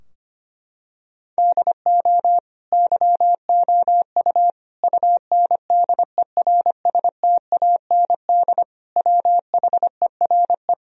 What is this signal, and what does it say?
Morse code, heard as a keyed tone, about 25 wpm, 700 Hz. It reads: DO YOU UNDERSTAND WHERE
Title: Sentences from Top 1000 Words - Review Part 4 - 25wpm